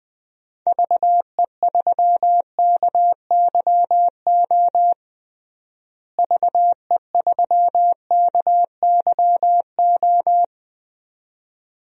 Morse code, heard as a keyed tone, about 20 wpm, 690 Hz. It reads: VE3KYO VE3KYO
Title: DXCC-100 Call Signs - 6 Characters - Repeated 2x - 20wpm